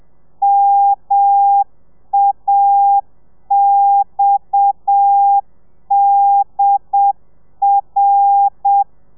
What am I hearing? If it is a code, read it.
Morse code, heard as a keyed tone, about 7 wpm, 800 Hz. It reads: MAXDR
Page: Lezione 1: audio a 30 CPM